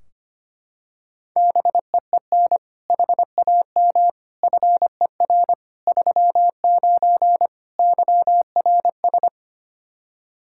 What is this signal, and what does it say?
Morse code, heard as a keyed tone, about 25 wpm, 700 Hz. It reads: BEEN HAM FER 39 YRS
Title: K1USN SST QSO Practice 25wpm